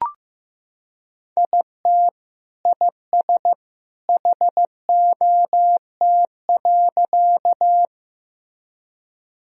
Morse code, heard as a keyed tone, about 15 wpm, 700 Hz. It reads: IT IS HOT.
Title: Sentences from Top 1000 Words - Review Part 3 - 15wpm